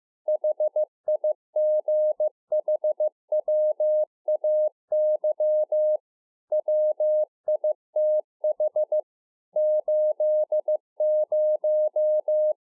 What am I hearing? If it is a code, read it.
Morse code, heard as a keyed tone, about 15 wpm, 610 Hz. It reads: HIGHWAY WITH 80